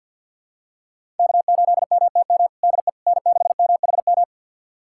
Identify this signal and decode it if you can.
Morse code, heard as a keyed tone, about 49 wpm, 700 Hz. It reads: K8MTM DE N6MHM
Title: USA Callsign DE Callsign - 5 Characters - 50wpm